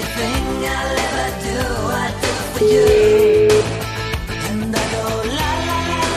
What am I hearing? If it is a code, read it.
Ringback tone (Europe, 425 Hz), with noise.